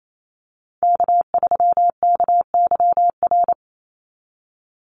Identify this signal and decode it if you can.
Morse code, heard as a keyed tone, about 28 wpm, 700 Hz. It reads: K3KYR
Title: NA SOTA Chaser Callsigns 28wpm